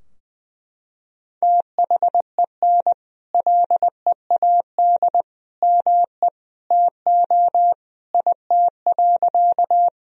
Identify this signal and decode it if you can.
Morse code, heard as a keyed tone, about 21 wpm, 700 Hz. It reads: THEN LEAD ME TO IT.
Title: Sentences from Top 1000 Words - Review Part 2 - 20wpm